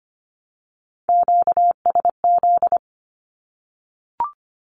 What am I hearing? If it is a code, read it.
Morse code, heard as a keyed tone, about 25 wpm, 700 Hz. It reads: QSZ
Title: Q-Codes 25wpm